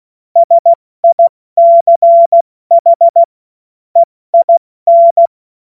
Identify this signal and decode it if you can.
Morse code, heard as a keyed tone, about 16 wpm, spaced as for 14 wpm, 680 Hz. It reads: SICH EIN
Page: cw-witze